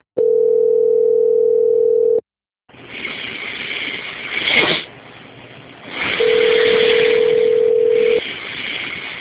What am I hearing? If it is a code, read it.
Ringback tone (North America), 440 + 480 Hz, with noise.
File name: out- - - - - .31952